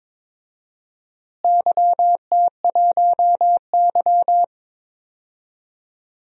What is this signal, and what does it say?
Morse code, heard as a keyed tone, about 22 wpm, 700 Hz. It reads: YT1Y
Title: DXCC-100 Call Signs - 4 Characters - Repeated 2x - 22wpm